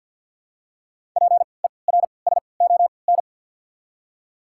Morse code, heard as a keyed tone, about 53 wpm, spaced as for 29 wpm, 690 Hz. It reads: PERSON
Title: 3x Character Spacing - Top 500 Words - 50wpm